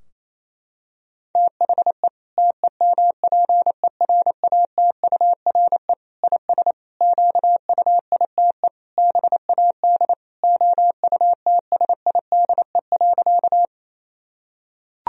Morse code, heard as a keyed tone, about 27 wpm, 700 Hz. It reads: THE TEMPERATURE IS QUITE BAD OUTSIDE.
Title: Sentences from Top 1000 Words - Review Part 4 - 28wpm